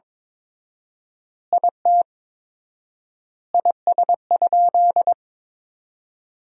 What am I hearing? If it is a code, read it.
Morse code, heard as a keyed tone, about 22 wpm, 700 Hz. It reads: IT IS?
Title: 4x Word-Spacing - Sentences from Top 100 Words - 22wpm